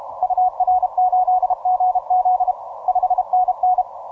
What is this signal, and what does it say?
Morse code, heard as a keyed tone, about 32 wpm, 740 Hz. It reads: UR8GZ 5NN